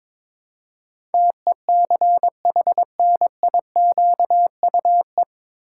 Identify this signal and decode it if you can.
Morse code, heard as a keyed tone, about 23 wpm, 700 Hz. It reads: TECHNIQUE
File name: 9 Letter Words - First 3 Letters Given - 22wpm